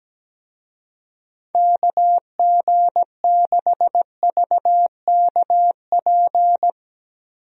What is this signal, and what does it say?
Morse code, heard as a keyed tone, about 17 wpm, 700 Hz. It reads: KG6VKP
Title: USA Call Signs - 6 Characters - 17wpm